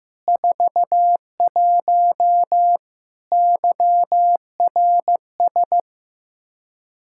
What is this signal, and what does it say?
Morse code, heard as a keyed tone, about 15 wpm, 700 Hz. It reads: 41 YRS
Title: Practice QSO - Exchange 3 - More Info - 15wpm